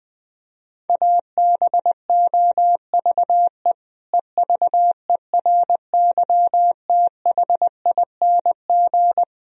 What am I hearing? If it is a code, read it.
Morse code, heard as a keyed tone, about 20 wpm, 700 Hz. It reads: ABOVE EVERYTHING
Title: Sentences from Top 1000 Words - Review Part 2 - 20wpm